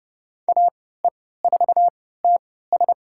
Morse code, heard as a keyed tone, about 30 wpm, spaced as for 17 wpm, 700 Hz. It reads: AE4TS
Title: USA Call Signs - 3x Character Spacing - 30wpm